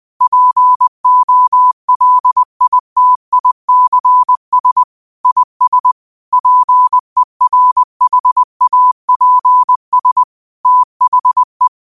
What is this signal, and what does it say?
Morse code, heard as a keyed tone, about 20 wpm, 1000 Hz. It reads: POLITICS IS PERHAPS THE
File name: qotdfiles / 20wpm